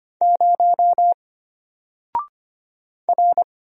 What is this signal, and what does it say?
Morse code, heard as a keyed tone, about 25 wpm, 700 Hz. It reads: 0 R
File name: Single-Letter-Number-25wpm